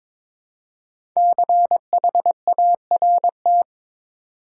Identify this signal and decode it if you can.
Morse code, heard as a keyed tone, about 22 wpm, 700 Hz. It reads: CHART
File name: 5 Letter Words - First 3 Letters Given - 22wpm